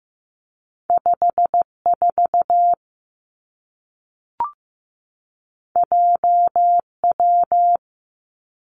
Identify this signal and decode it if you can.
Morse code, heard as a keyed tone, about 15 wpm, 700 Hz. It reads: 5 4 J W
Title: Two Challenging Letter-Number Combinations 15wpm